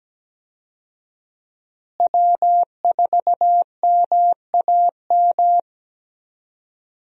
Morse code, heard as a keyed tone, about 17 wpm, 700 Hz. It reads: W4MAM
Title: USA Call Signs - 5 Characters - Repeated 2x - 17wpm